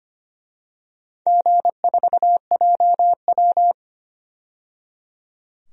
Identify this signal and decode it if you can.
Morse code, heard as a keyed tone, about 25 wpm, 700 Hz. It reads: G4JW
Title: DXCC-100 Call Signs - 4 Characters - 25wpm